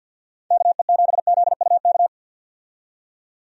Morse code, heard as a keyed tone, about 49 wpm, 700 Hz. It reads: KE7ZUK
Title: USA Call Signs - 6 Characters - 50wpm